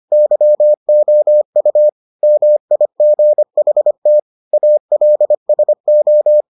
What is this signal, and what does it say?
Morse code, heard as a keyed tone, about 25 wpm, 600 Hz. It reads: YOU MIGHT ALSO